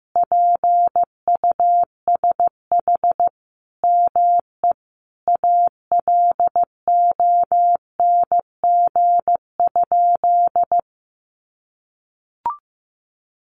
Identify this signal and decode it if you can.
Morse code, heard as a keyed tone, about 15 wpm, 700 Hz. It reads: PUSH ME ALONG?
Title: Sentences from Top 1300 Words 15wpm